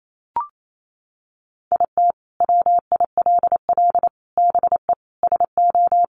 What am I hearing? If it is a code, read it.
Morse code, heard as a keyed tone, about 28 wpm, 700 Hz. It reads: IT WILL BE SO
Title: Sentences from Top 1500 Words 28wpm